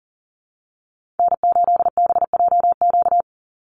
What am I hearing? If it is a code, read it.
Morse code, heard as a keyed tone, about 40 wpm, 700 Hz. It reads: N8BJQ
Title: Call Signs Repeated 3x - 40wpm